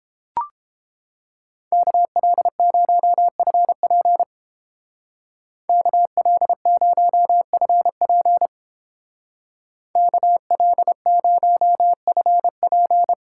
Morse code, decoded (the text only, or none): KL0FP KL0FP KL0FP